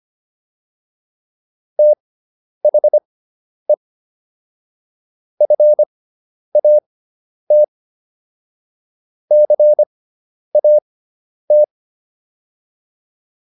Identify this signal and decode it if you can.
Morse code, heard as a keyed tone, about 25 wpm, spaced as for 10 wpm, 600 Hz. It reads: THE FAT CAT